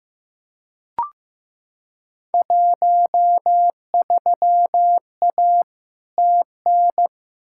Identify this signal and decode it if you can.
Morse code, heard as a keyed tone, about 15 wpm, 700 Hz. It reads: 13A TN
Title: Field Day Exchanges Encore - 15wpm